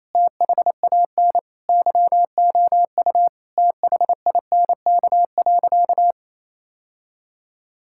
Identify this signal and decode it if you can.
Morse code, heard as a keyed tone, about 28 wpm, 700 Hz. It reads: THAN YOU THINK.
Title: Sentences from Top 1000 Words - Review Part 6 - 28wpm